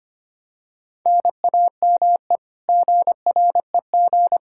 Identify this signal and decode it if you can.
Morse code, heard as a keyed tone, about 25 wpm, 700 Hz. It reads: NAME GREG
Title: Practice QSO - Exchange 1 - Beginning - 25wpm